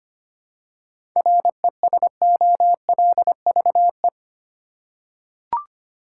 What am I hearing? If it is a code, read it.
Morse code, heard as a keyed tone, about 25 wpm, 700 Hz. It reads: RESOLVE
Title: Seven Letter Words - 25wpm